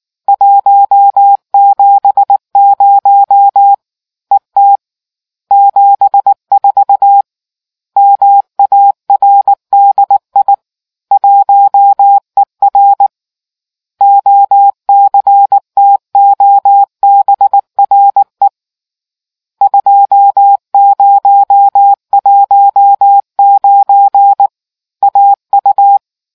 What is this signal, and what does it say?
Morse code, heard as a keyed tone, about 19 wpm, 800 Hz. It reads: 170 ET 74 MARDI 1ER OCTOBRE 2019 AU